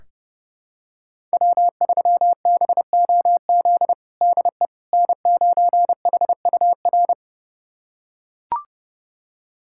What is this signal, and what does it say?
Morse code, heard as a keyed tone, about 30 wpm, 700 Hz. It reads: W3BOZ DE N9HUR
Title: USA Callsign DE Callsign - 5 Characters - 30wpm